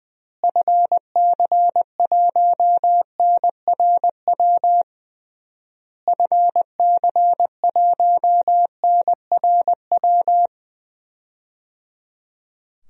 Morse code, heard as a keyed tone, about 20 wpm, 700 Hz. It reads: FC1NRW FC1NRW
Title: DXCC-100 Call Signs - 6 Characters - Repeated 3x - 20wpm